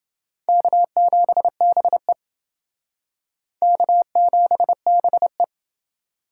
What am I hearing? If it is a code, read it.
Morse code, heard as a keyed tone, about 28 wpm, 700 Hz. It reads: K7BE K7BE